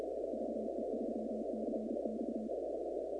Morse code, heard as a keyed tone, about 32 wpm, 230 Hz. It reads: VE3CX
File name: contests / 2019-cw-wpx-kt5j-k5pi- -2330 0-19